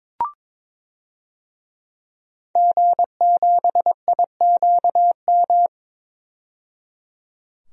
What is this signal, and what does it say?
Morse code, heard as a keyed tone, about 22 wpm, 700 Hz. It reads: G7IQM
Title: DXCC-100 Call Signs - 5 Characters - 22wpm